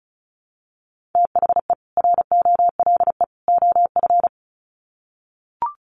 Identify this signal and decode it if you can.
Morse code, heard as a keyed tone, about 34 wpm, 700 Hz. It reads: THE ROLE OF
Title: Sets of 3 Words 35wpm